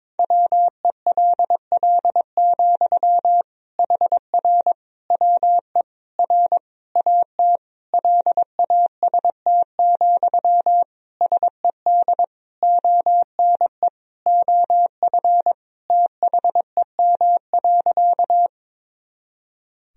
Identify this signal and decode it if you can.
Morse code, heard as a keyed tone, about 22 wpm, 700 Hz. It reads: WELL, HR WE R AT LAST, SED ONE OF THEM.
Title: Sentences from Top 300 Words and Abbreviations 22wpm